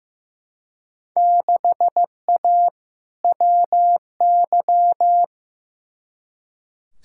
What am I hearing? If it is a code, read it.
Morse code, heard as a keyed tone, about 15 wpm, 700 Hz. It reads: 6A WY
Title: Field Day Exchanges Encore - 15wpm